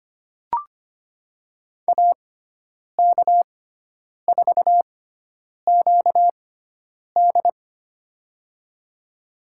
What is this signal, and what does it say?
Morse code, heard as a keyed tone, about 25 wpm, spaced as for 9 wpm, 700 Hz. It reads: AK4QD
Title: USA Call Signs - 6x Character Spacing - 25wpm